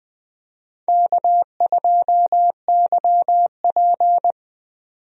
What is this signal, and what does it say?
Morse code, heard as a keyed tone, about 20 wpm, 700 Hz. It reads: K2YP
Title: USA Call Signs - 4 Characters - 20wpm